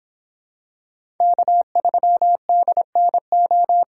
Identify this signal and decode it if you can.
Morse code, heard as a keyed tone, about 26 wpm, 700 Hz. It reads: K3DNO